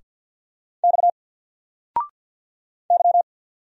Morse code, heard as a keyed tone, about 50 wpm, 700 Hz. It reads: X Y
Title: Single Letter-Number 50wpm